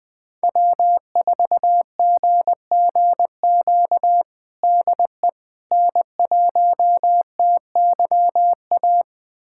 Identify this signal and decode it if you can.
Morse code, heard as a keyed tone, about 20 wpm, 700 Hz. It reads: W4GGQ DE N1TYA
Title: USA Callsign DE Callsign - 5 Characters - 20wpm